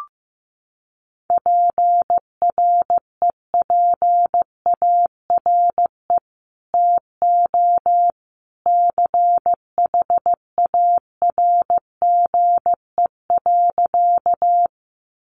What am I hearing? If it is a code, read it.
Morse code, heard as a keyed tone, about 15 wpm, 700 Hz. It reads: PREPARE TO CHARGE.
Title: Sentences from Top 900 Words 15wpm